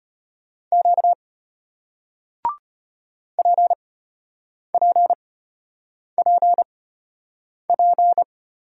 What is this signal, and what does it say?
Morse code, heard as a keyed tone, about 33 wpm, spaced as for 8 wpm, 700 Hz. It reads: Q PPPP